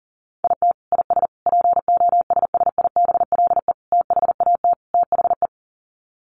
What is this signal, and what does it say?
Morse code, heard as a keyed tone, about 40 wpm, 700 Hz. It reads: IT IS POSSIBLE THAT THE
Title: Sets of 5 Words Encore2 40wpm